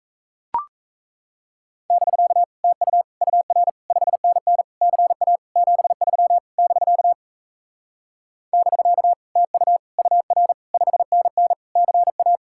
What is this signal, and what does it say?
Morse code, heard as a keyed tone, about 36 wpm, 700 Hz. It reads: <BK> TU UR 5NN CA 73 <BK> <BK> TU UR 5NN CA